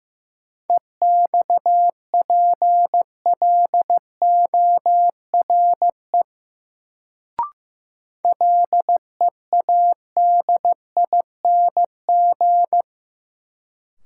Morse code, heard as a keyed tone, about 15 wpm, 700 Hz. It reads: EXPLORE LEADING
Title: Sentences from Top 1100 Words - New Word Drill - 15wpm